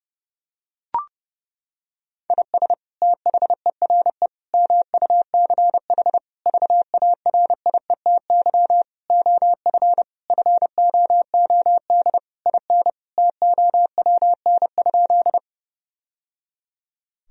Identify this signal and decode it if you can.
Morse code, heard as a keyed tone, about 30 wpm, 700 Hz. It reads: IS THERE MUCH VARIETY OF FOOD IN TOWN?
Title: Sentences from Top 1000 Words - Review Part 4 - 30wpm